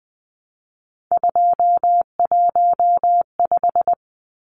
Morse code, heard as a keyed tone, about 20 wpm, 700 Hz. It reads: 215
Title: Numbers - 3-Digit 20wpm